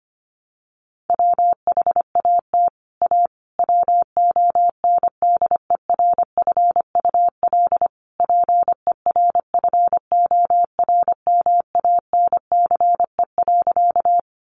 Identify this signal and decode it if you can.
Morse code, heard as a keyed tone, about 25 wpm, 700 Hz. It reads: WHAT A WONDERFUL PERFORMANCE.
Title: Sentences from Top 1100 Words 25wpm